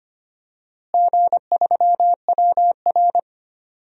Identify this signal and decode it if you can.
Morse code, heard as a keyed tone, about 25 wpm, 700 Hz. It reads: G3WR
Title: DXCC-100 Call Signs - 4 Characters - Repeated 3x - 25wpm